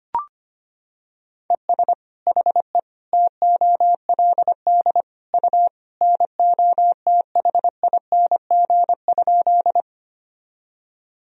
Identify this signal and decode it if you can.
Morse code, heard as a keyed tone, about 25 wpm, 700 Hz. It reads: ES HE TOLD U NOTHING?
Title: Sentences from Top 500 Words and Abbreviations 25wpm